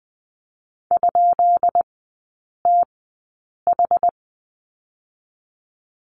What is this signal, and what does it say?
Morse code, heard as a keyed tone, about 20 wpm, spaced as for 8 wpm, 700 Hz. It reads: ?TH
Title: 145 Lesson 24e - Letter y - Three Characters 20wpm